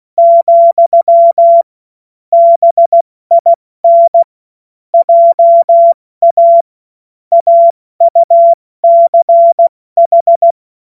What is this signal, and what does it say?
Morse code, heard as a keyed tone, about 16 wpm, 680 Hz. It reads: , BIN JA AUCH
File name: Morsekurs / Witze / Witz 19 Dezember 2023 70 F80